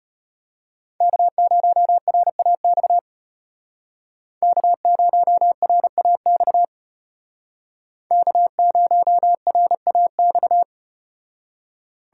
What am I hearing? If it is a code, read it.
Morse code, heard as a keyed tone, about 35 wpm, 700 Hz. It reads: K0RAX K0RAX K0RAX